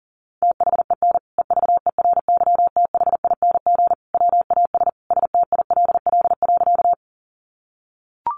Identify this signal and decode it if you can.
Morse code, heard as a keyed tone, about 40 wpm, 700 Hz. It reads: THEN EVERYTHING WAS STILL.
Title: Sentences from Top 700 Words 40wpm